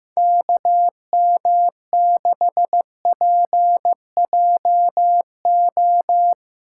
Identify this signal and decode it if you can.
Morse code, heard as a keyed tone, about 17 wpm, 700 Hz. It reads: KM6PJO
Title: Speed Racing - USA Call Signs 15wpm